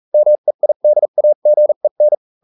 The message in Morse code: MEIDAGEN